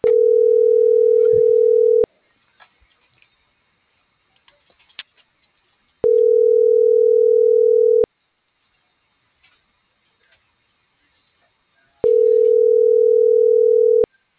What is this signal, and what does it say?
Ringback tone (North America), 440 + 480 Hz.